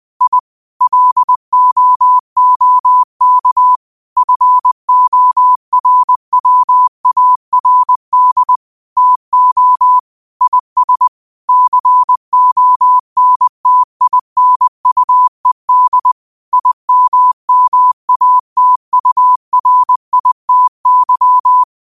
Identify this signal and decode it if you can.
Morse code, heard as a keyed tone, about 20 wpm, 1000 Hz. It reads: I LOOK FORWARD TO IS CONTINUED IMMATURITY
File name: qotdfiles / 20wpm